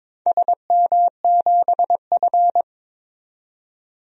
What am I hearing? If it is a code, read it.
Morse code, heard as a keyed tone, about 22 wpm, 700 Hz. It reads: SM7F
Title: DXCC-100 Call Signs - 4 Characters - 22wpm